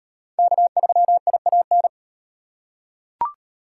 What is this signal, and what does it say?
Morse code, heard as a keyed tone, about 38 wpm, 700 Hz. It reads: K3IAN